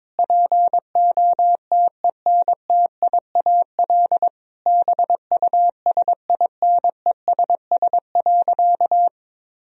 Morse code, heard as a keyed tone, about 22 wpm, 700 Hz. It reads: POTENTIAL BUSINESS.
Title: Sentences from Top 1000 Words - Review Part 4 - 22wpm